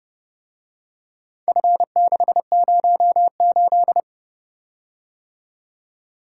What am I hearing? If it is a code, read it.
Morse code, heard as a keyed tone, about 30 wpm, 700 Hz. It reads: F608